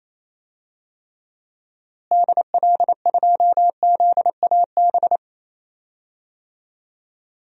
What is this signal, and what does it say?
Morse code, heard as a keyed tone, about 28 wpm, 700 Hz. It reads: DL2ZAB
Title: DXCC-100 Call Signs - 6 Characters - Repeated 3x - 28wpm